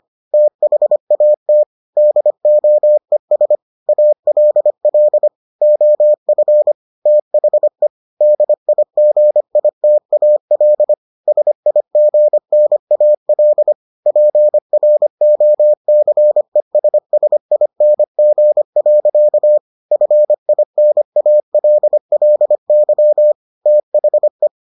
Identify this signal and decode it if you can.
Morse code, heard as a keyed tone, about 25 wpm, 600 Hz. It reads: THAT DOES ALL OF THE DIGITAL SIGNAL PROCESSING. FINALLY THE